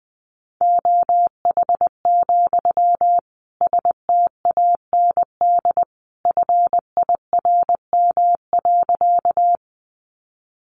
Morse code, heard as a keyed tone, about 20 wpm, 700 Hz. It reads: OH, STAND FIRM.
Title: Sentences from Top 1400 Words 20wpm